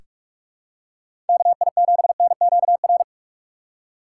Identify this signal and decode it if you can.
Morse code, heard as a keyed tone, about 43 wpm, 700 Hz. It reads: KI7NQR
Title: USA Call Signs - 6 Characters - 2x Character Spacing - 45wpm